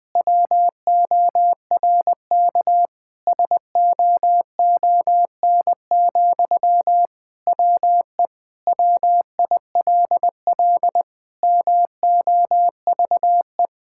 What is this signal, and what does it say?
Morse code, heard as a keyed tone, about 20 wpm, 700 Hz. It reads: WORK SOON, WE WILL MOVE
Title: Sentences from Top 1000 Words - Review Part 6 - 20wpm